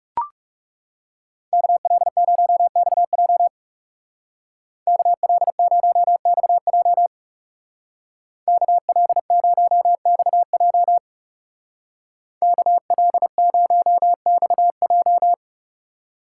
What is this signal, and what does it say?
Morse code, heard as a keyed tone, about 35 wpm, 700 Hz. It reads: KL0XJ KL0XJ KL0XJ KL0XJ
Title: Speed Racing - USA 2x2 Call Signs 30wpm